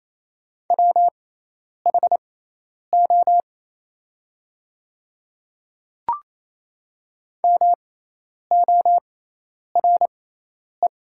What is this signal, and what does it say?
Morse code, heard as a keyed tone, about 28 wpm, spaced as for 10 wpm, 690 Hz. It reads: WHO MORE
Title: 6x Character Spacing - Top 100 Words - 28wpm